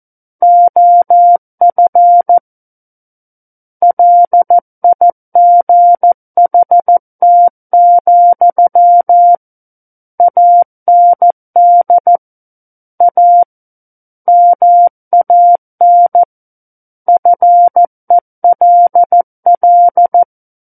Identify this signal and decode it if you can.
Morse code, heard as a keyed tone, about 14 wpm, 700 Hz. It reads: OF LIGHT, AND A MAN FELL
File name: Book One / 05-Chapter 5 - 14 WPM